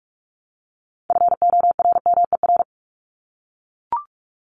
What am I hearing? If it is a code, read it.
Morse code, heard as a keyed tone, about 44 wpm, 700 Hz. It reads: FORMER
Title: Top 1751-2000 Words 45wpm